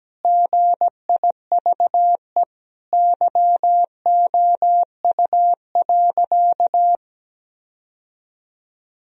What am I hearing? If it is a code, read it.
Morse code, heard as a keyed tone, about 17 wpm, 700 Hz. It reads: GIVE YOU.
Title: Sentences from Top 1000 Words - Review Part 2 - 17wpm